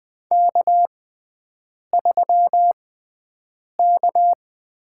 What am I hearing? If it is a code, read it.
Morse code, heard as a keyed tone, about 20 wpm, spaced as for 7 wpm, 700 Hz. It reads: K3K
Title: USA Call Signs - 5 Characters - 6x Character Spacing - 20wpm